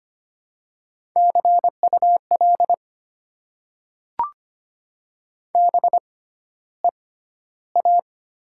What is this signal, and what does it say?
Morse code, heard as a keyed tone, about 25 wpm, 700 Hz. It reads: C U L B E A
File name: Top 90 QSO Elements - 6x Character Spacing - 25wpm